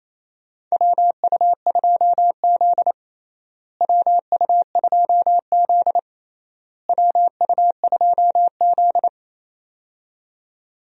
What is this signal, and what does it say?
Morse code, heard as a keyed tone, about 28 wpm, 700 Hz. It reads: WU2Z WU2Z WU2Z
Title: USA Call Signs - 4 Characters - Repeated 3x - 28wpm